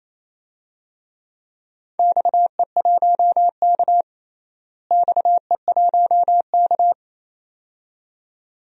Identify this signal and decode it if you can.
Morse code, heard as a keyed tone, about 28 wpm, 700 Hz. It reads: XE1K XE1K
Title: DXCC-100 Call Signs - 4 Characters - Repeated 2x - 28wpm